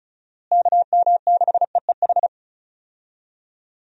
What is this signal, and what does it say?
Morse code, heard as a keyed tone, about 34 wpm, 700 Hz. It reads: KM6EEH
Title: USA Call Signs - 6 Characters - Repeated 2x - 35wpm